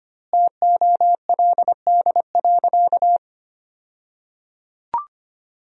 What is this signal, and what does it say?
Morse code, heard as a keyed tone, about 25 wpm, 700 Hz. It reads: TOLD.
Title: 2x Word-Spacing - Sentences from Top 400 Words - 25wpm